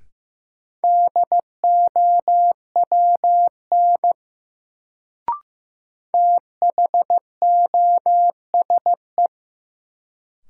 Morse code, heard as a keyed tone, about 15 wpm, 700 Hz. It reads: DOWN THOSE
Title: Sentences from Top 300 Words - New Word Drill - 15wpm